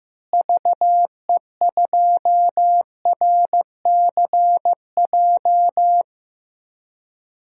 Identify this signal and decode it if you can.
Morse code, heard as a keyed tone, about 15 wpm, 700 Hz. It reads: VE2RCJ
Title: DXCC-100 Call Signs - 6 Characters - Repeated 2x - 15wpm